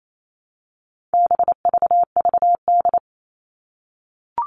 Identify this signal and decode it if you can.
Morse code, heard as a keyed tone, about 28 wpm, 700 Hz. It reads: BVVD
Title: All Cut Numbers - 4-Digit 28wpm